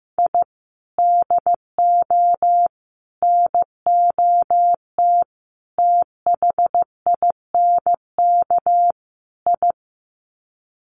Morse code, heard as a keyed tone, about 15 wpm, 700 Hz. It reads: I DO NOT THINK I
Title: Sets of 5 Words 15wpm